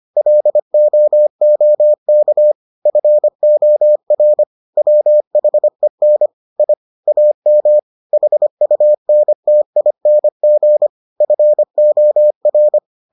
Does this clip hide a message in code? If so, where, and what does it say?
Yes: Morse code from 0.2 to 12.8 s, heard as a keyed tone, about 25 wpm, 600 Hz. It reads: LOOK FOR WHEN I AM HUNTING FOR